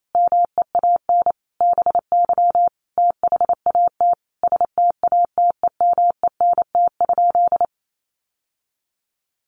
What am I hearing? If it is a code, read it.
Morse code, heard as a keyed tone, about 28 wpm, 700 Hz. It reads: MEAN BY THAT STATEMENT?
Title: Sentences from Top 1200 Words 28wpm